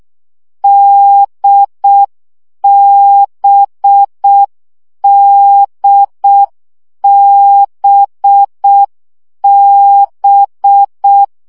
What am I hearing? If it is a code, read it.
Morse code, heard as a keyed tone, about 6 wpm, 800 Hz. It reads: DBDBB
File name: CW gyakorlas d b x 5 baud